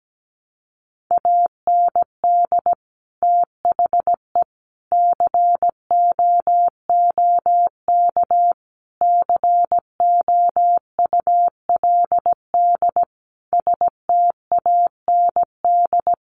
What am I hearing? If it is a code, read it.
Morse code, heard as a keyed tone, about 17 wpm, 700 Hz. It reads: AND THE COOK COULD STAND
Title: Sentences from Top 1300 Words 17wpm